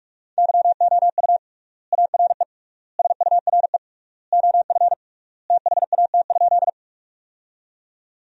Morse code, heard as a keyed tone, about 46 wpm, 700 Hz. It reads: YOU ARE SURE OF THAT?
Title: 3x Word-Spacing - Sentences from Top 300 Words - 45wpm